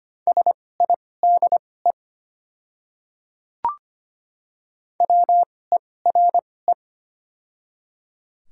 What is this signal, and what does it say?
Morse code, heard as a keyed tone, about 25 wpm, spaced as for 18 wpm, 690 Hz. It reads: SIDE WERE
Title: 2x Character Spacing - Top 300 Words - 25wpm